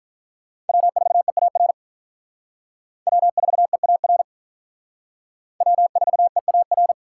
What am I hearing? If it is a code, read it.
Morse code, heard as a keyed tone, about 43 wpm, 700 Hz. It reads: W4EAR W4EAR W4EAR
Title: Speed Racing - USA 1x3 Call Signs 35wpm